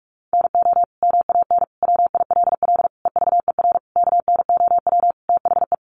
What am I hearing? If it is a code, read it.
Morse code, heard as a keyed tone, about 44 wpm, 700 Hz. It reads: NO MAN WILL EVER KNOW THE
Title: Sentences from Top 1300 Words 45wpm